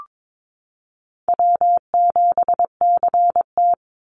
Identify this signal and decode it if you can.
Morse code, heard as a keyed tone, about 22 wpm, 700 Hz. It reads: W7CT
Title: Easy Call Signs Repeated 3x - 22wpm